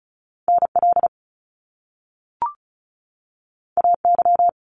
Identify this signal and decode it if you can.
Morse code, heard as a keyed tone, about 36 wpm, 700 Hz. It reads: NL AY
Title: Top Two Letter Combinations in Words 35wpm